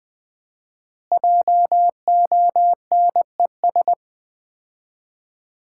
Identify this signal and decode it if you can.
Morse code, heard as a keyed tone, about 20 wpm, 700 Hz. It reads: JONES
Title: 4x Word-Spacing - Sets of 5 Words - 20wpm